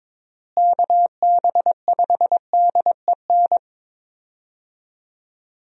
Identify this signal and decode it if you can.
Morse code, heard as a keyed tone, about 21 wpm, 700 Hz. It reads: KB5DEN
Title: USA Call Signs - 6 Characters - Repeated 2x - 22wpm